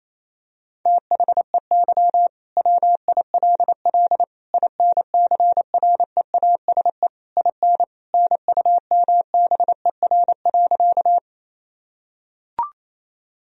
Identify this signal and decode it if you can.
Morse code, heard as a keyed tone, about 28 wpm, 700 Hz. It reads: THEY WILL INCREASE IN NUMBER.
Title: Sentences from Top 1000 Words - Review Part 4 - 28wpm